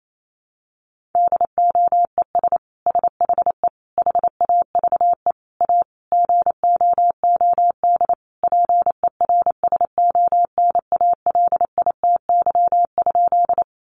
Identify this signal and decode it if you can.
Morse code, heard as a keyed tone, about 28 wpm, 700 Hz. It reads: DOES SHE HAVE A GOOD PERSONALITY?
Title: Sentences from Top 1400 Words 28wpm